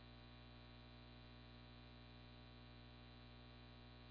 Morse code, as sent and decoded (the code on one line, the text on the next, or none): none